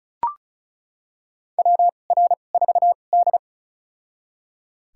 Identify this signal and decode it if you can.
Morse code, heard as a keyed tone, about 36 wpm, spaced as for 25 wpm, 700 Hz. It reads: WR4D
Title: USA Call Signs - 4 Characters - 2x Character Spacing - 35wpm